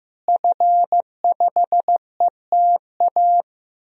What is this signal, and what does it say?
Morse code, heard as a keyed tone, about 15 wpm, 700 Hz. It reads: F5ETA
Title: DXCC-100 Call Signs - 5 Characters - Repeated 2x - 15wpm